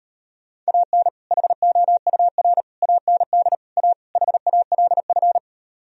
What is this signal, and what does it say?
Morse code, heard as a keyed tone, about 32 wpm, 700 Hz. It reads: AN HOUR AND A HALF